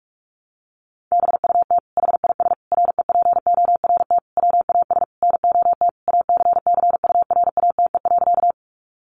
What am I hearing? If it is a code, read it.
Morse code, heard as a keyed tone, about 44 wpm, 700 Hz. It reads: BUT HIS REPORT WAS NOT ACCURATE.
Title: Sentences from Top 1200 Words 45wpm